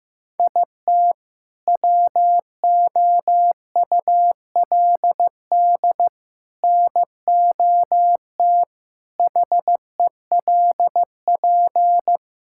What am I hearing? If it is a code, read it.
Morse code, heard as a keyed tone, about 15 wpm, 700 Hz. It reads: IT WOULD NOT HELP
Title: Sentences from Top 1600 Words 15wpm